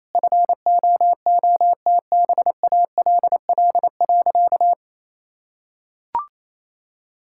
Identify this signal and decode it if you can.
Morse code, heard as a keyed tone, about 28 wpm, 700 Hz. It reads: FOOTBALL.
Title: Sentences from Top 1600 Words 28wpm